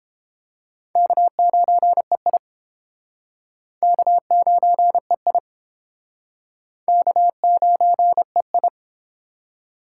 Morse code, heard as a keyed tone, about 30 wpm, 700 Hz. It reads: K9EI K9EI K9EI